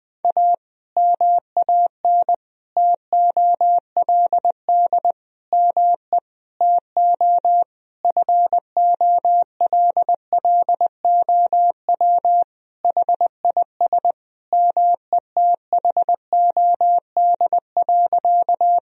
Morse code, heard as a keyed tone, about 20 wpm, 700 Hz. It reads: A MAN TOLD ME TO FOLLOW HIS METHOD.
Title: Sentences from Top 1000 Words - Review Part 3 - 20wpm